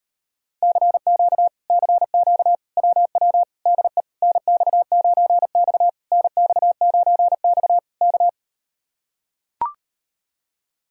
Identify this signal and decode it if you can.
Morse code, heard as a keyed tone, about 39 wpm, 700 Hz. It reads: CQ CQ WW DE NX9X NX9X K